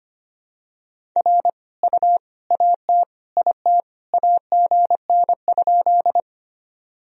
Morse code, heard as a keyed tone, about 25 wpm, 700 Hz. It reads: R U AT IT AGN?
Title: Sentences from Top 200 Words and Abbreviations 25wpm